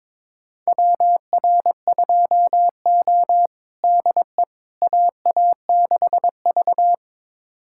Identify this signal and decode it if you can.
Morse code, heard as a keyed tone, about 22 wpm, 700 Hz. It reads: WR2O DE AA6V
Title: USA Callsign DE Callsign - 4 Characters - 22wpm